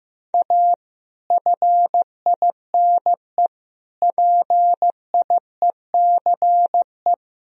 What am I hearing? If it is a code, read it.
Morse code, heard as a keyed tone, about 15 wpm, 700 Hz. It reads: A FINE PIECE
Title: Sentences from Top 1000 Words - Review Part 2 - 15wpm